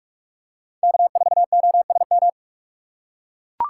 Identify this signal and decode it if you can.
Morse code, heard as a keyed tone, about 41 wpm, 700 Hz. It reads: K4OSM